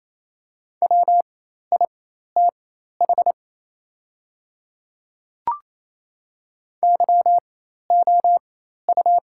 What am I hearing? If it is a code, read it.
Morse code, heard as a keyed tone, about 28 wpm, spaced as for 13 wpm, 700 Hz. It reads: WITH YOU
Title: 4x Character Spacing - Top 200 Words - 28wpm